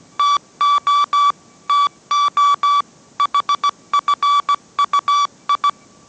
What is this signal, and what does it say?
Morse code, heard as a keyed tone, about 17 wpm, 1150 Hz. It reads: TO TO HFUI